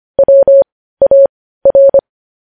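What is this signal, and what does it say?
Morse code, heard as a keyed tone, about 25 wpm, spaced as for 15 wpm, 550 Hz. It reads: WAR
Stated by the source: Morse practice sending news headlines at 15 wpm.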